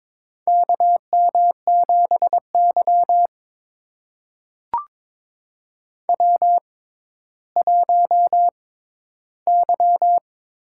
Morse code, heard as a keyed tone, about 22 wpm, spaced as for 13 wpm, 700 Hz. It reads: KM7Y W1Y
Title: USA Call Signs - 4 Characters - 6x Character Spacing - 22wpm